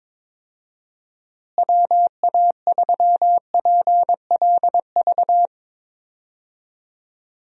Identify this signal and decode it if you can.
Morse code, heard as a keyed tone, about 22 wpm, 700 Hz. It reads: WA3PLV
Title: USA Call Signs - 6 Characters - Repeated 2x - 22wpm